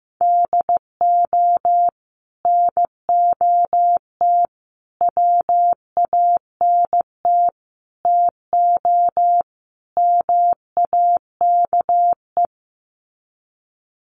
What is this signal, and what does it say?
Morse code, heard as a keyed tone, about 15 wpm, 700 Hz. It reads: DO NOT WANT TO MAKE
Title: Sets of 5 Words Encore 15wpm